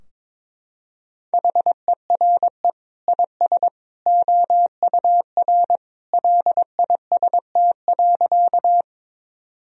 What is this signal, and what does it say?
Morse code, heard as a keyed tone, about 21 wpm, 700 Hz. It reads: HERE IS OUR LIST.
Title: Sentences from Top 1000 Words - Review Part 2 - 22wpm